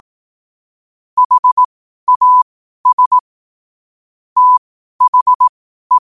Morse code, heard as a keyed tone, about 18 wpm, spaced as for 12 wpm, 1000 Hz. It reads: HAS THE
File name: qotdfiles / 12wpm